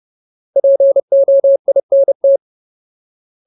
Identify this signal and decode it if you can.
Morse code, heard as a keyed tone, about 30 wpm, 550 Hz. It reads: POINT
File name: Ultimate Word List / 30WPM Ultimate